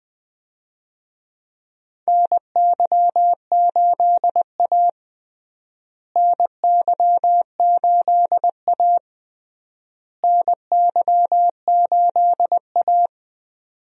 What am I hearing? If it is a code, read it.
Morse code, heard as a keyed tone, about 21 wpm, 700 Hz. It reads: NY8A NY8A NY8A
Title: USA Call Signs - 4 Characters - Repeated 3x - 20wpm